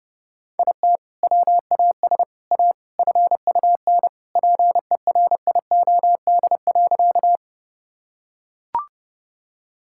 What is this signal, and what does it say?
Morse code, heard as a keyed tone, about 32 wpm, 700 Hz. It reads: IT WAS A FUN PERIOD.
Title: Sentences from Top 1000 Words - Review Part 5 - 30wpm